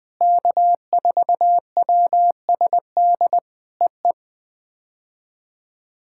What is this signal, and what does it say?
Morse code, heard as a keyed tone, about 19 wpm, 700 Hz. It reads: K4WSD EE
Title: POTA QSO Practice - 20wpm